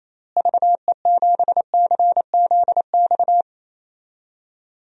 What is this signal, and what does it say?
Morse code, heard as a keyed tone, about 28 wpm, 700 Hz. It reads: VE7CZX
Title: DXCC-100 Call Signs - 6 Characters - Repeated 2x - 28wpm